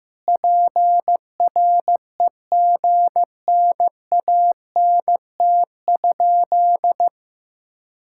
Morse code, heard as a keyed tone, about 16 wpm, 700 Hz. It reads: PREGNANT?
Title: Sentences from Top 1000 Words - Review Part 6 - 15wpm